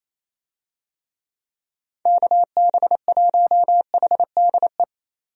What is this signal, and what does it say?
Morse code, heard as a keyed tone, about 28 wpm, 700 Hz. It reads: KB1HDE
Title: USA Call Signs - 6 Characters - 28wpm